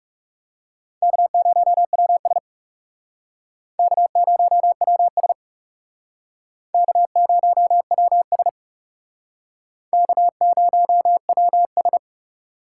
Morse code, heard as a keyed tone, about 37 wpm, 700 Hz. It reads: K0WS K0WS K0WS K0WS
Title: Speed Racing - USA 1x2 Call Signs 30wpm